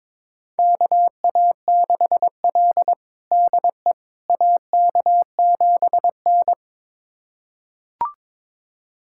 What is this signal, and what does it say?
Morse code, heard as a keyed tone, about 22 wpm, 700 Hz. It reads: KA6L DE AK7N
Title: USA Callsign DE Callsign - 4 Characters - 22wpm